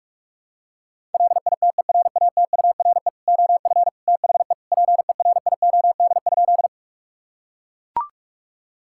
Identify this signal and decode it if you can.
Morse code, heard as a keyed tone, about 41 wpm, 690 Hz. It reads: LITERATURE OF THE PERIOD?
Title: Sentences from Top 1000 Words - Review Part 5 - 45wpm